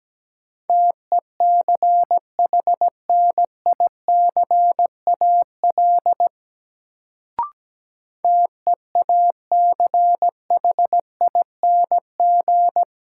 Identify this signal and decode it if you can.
Morse code, heard as a keyed tone, about 17 wpm, 700 Hz. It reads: TECHNICAL TEACHING
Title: Sentences from Top 1100 Words - New Word Drill - 17wpm